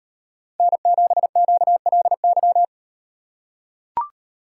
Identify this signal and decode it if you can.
Morse code, heard as a keyed tone, about 36 wpm, 700 Hz. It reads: N7QLY